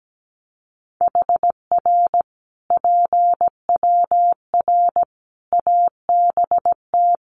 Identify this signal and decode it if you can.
Morse code, heard as a keyed tone, about 17 wpm, 700 Hz. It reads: HR PWR ABT
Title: Common QSO Phrases 17wpm